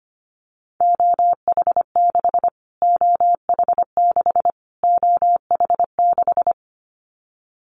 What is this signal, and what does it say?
Morse code, heard as a keyed tone, about 25 wpm, 700 Hz. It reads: OH6 OH6 OH6
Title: Callsign Prefixes for Top 25 DXCC - Repeated 3x - 25wpm